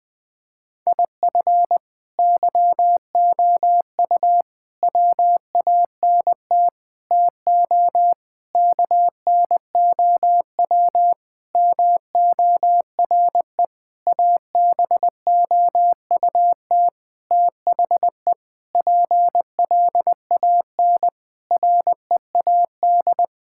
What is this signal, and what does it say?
Morse code, heard as a keyed tone, about 20 wpm, 700 Hz. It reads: IF YOU WANT TO KNOW MORE ABOUT THE PLAN READ
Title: Sentences with 8-Word Phrases - Spoken Beginning - 20wpm